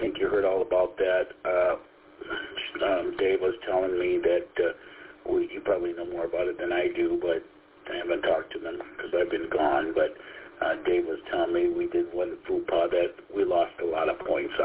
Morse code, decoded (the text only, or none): none